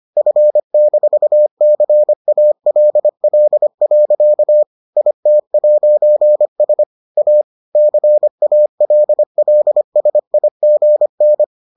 Morse code, heard as a keyed tone, about 25 wpm, 600 Hz. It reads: F-CALL. IT'S A CALLSIGN